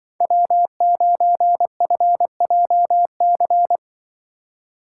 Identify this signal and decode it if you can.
Morse code, heard as a keyed tone, about 24 wpm, 700 Hz. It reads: W9FJC
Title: Speed Racing - USA 1x3 Call Signs 20wpm